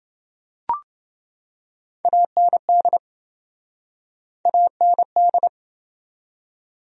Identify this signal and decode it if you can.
Morse code, heard as a keyed tone, about 27 wpm, 700 Hz. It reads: AND AND